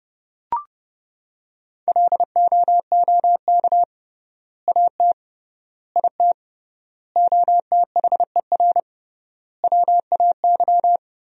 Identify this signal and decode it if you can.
Morse code, heard as a keyed tone, about 30 wpm, 700 Hz. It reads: LOOK AT IT OTHER WAY
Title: 3x Word-Spacing - Sentences from Top 200 Words - 30wpm